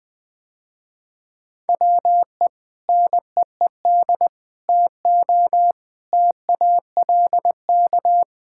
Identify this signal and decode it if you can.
Morse code, heard as a keyed tone, about 20 wpm, 700 Hz. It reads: WE NEED TO TALK
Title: Sentences with 3-Word Phrases - Spoken Beginning - 20wpm